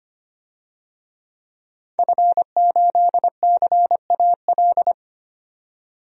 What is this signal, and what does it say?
Morse code, heard as a keyed tone, about 25 wpm, 700 Hz. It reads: F8CAL
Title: DXCC-100 Call Signs - 5 Characters - 25wpm